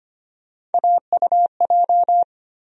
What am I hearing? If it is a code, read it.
Morse code, heard as a keyed tone, about 25 wpm, 700 Hz. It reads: AUJ
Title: USA Call Sign Postfixes - 3 Letters 25wpm